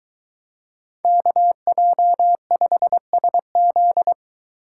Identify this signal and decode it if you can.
Morse code, heard as a keyed tone, about 23 wpm, 700 Hz. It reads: KJ5SZ